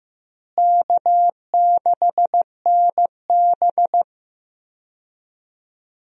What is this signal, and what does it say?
Morse code, heard as a keyed tone, about 15 wpm, 700 Hz. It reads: K6NB
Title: USA Call Signs - 4 Characters - Repeated 2x - 15wpm